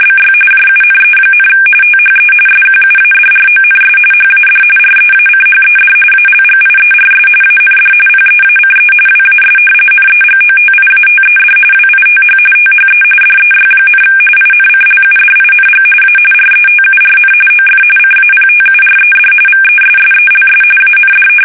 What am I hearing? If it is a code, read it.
RTTY, about 45.45 baud. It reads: BY JUNE, THE SQUADRONS HAD REACHED ALGIERS AND PEACE WAS NEGOTIATED WITH THE DEY, INCLUDING A RETURN OF CAPTURED VESSELS AND
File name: fsk850 / usnavy2